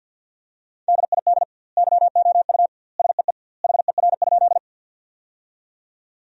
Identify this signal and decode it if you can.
Morse code, heard as a keyed tone, about 50 wpm, 700 Hz. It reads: DID YOU SEE HER?
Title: 2x Word-Spacing - Sentences from Top 100 Words - 50wpm